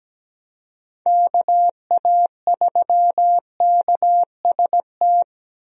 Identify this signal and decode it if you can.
Morse code, heard as a keyed tone, about 17 wpm, 700 Hz. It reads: KA3KST
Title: USA Call Signs - 6 Characters - Repeated 3x - 17wpm